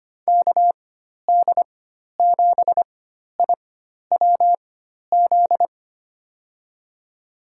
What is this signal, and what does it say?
Morse code, heard as a keyed tone, about 25 wpm, spaced as for 12 wpm, 700 Hz. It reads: KD7IWZ
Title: USA Call Signs - 6 Characters - 4x Character Spacing - 25wpm